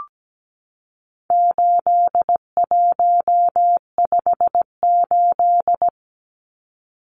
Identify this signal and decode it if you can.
Morse code, heard as a keyed tone, about 17 wpm, 700 Hz. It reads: 8158
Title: Numbers - 4-Digit 17wpm